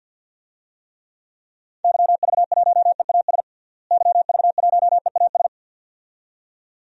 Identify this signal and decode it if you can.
Morse code, heard as a keyed tone, about 46 wpm, 700 Hz. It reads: YV1EAS YV1EAS
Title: DXCC-100 Call Signs - 6 Characters - Repeated 2x - 50wpm